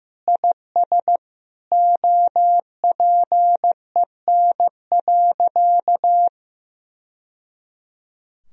Morse code, heard as a keyed tone, about 15 wpm, 700 Hz. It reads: IS OPEN.
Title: Sentences from Top 1000 Words - Review Part 1 - 15wpm